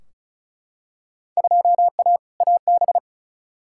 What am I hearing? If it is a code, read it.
Morse code, heard as a keyed tone, about 36 wpm, 700 Hz. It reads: 2A AB
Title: Field Day Exchanges - 35wpm